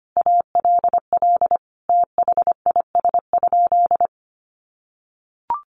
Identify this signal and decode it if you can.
Morse code, heard as a keyed tone, about 25 wpm, 700 Hz. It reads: ALL THIS?
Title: Sentences from Top 900 Words 25wpm